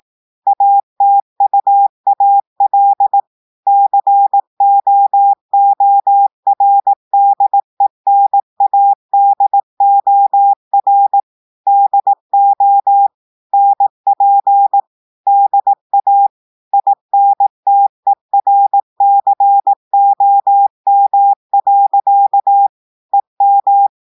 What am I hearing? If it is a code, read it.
Morse code, heard as a keyed tone, about 18 wpm, 800 Hz. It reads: ATUAL COORDENADOR DO NP DA INTERCOM. EM
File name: audios / portugais / portugais texte en clair 1 vitesse 18 mots minutes